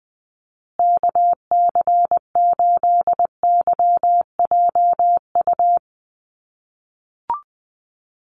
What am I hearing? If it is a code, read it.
Morse code, heard as a keyed tone, about 20 wpm, 700 Hz. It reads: KC8YJU
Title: USA Call Signs Repeated 3x 20wpm